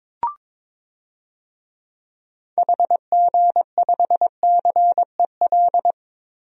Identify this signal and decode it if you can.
Morse code, heard as a keyed tone, about 22 wpm, 700 Hz. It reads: HG5CEL
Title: DXCC-100 Call Signs - 6 Characters - 22wpm